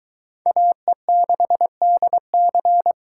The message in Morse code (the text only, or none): AE6DC